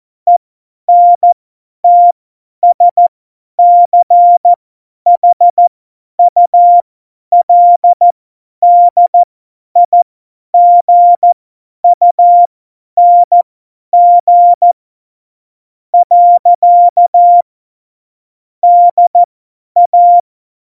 Morse code, heard as a keyed tone, about 14 wpm, spaced as for 10 wpm, 700 Hz. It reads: ENTSCHULDIGUNG . DA
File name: Morsekurs / Witze / Witz 11. Dezember, 50 F70